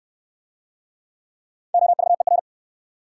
Morse code, heard as a keyed tone, about 58 wpm, 700 Hz. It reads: OVER